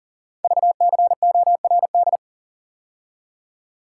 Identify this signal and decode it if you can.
Morse code, heard as a keyed tone, about 38 wpm, 700 Hz. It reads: VCORD
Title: Code Groups - 5 Letters - 40wpm